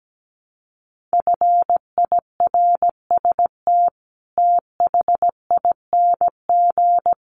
Morse code, heard as a keyed tone, about 17 wpm, 700 Hz. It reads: FIRST THING
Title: Sets of 5 Words Encore2 17wpm